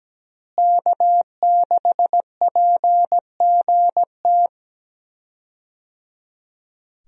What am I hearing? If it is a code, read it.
Morse code, heard as a keyed tone, about 17 wpm, 700 Hz. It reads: K6PGT
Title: USA Call Signs - 5 Characters - Repeated 3x - 17wpm